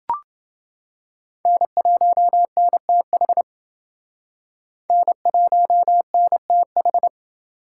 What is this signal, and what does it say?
Morse code, heard as a keyed tone, about 29 wpm, 700 Hz. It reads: N1NTH N1NTH